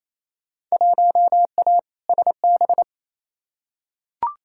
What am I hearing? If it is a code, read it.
Morse code, heard as a keyed tone, about 28 wpm, 700 Hz. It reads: 1A SB
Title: Field Day Exchanges Encore - 28wpm